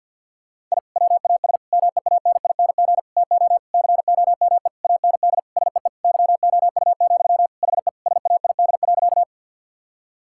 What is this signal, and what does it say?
Morse code, heard as a keyed tone, about 50 wpm, 700 Hz. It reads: I WAS MEANING TO COME AND SEE YOU, HE SAID.
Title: Sentences from Top 1000 Words - Review Part 6 - 50wpm